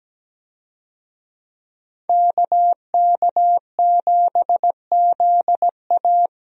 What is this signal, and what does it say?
Morse code, heard as a keyed tone, about 17 wpm, 700 Hz. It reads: KK7ZA
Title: USA Call Signs - 5 Characters - Repeated 3x - 17wpm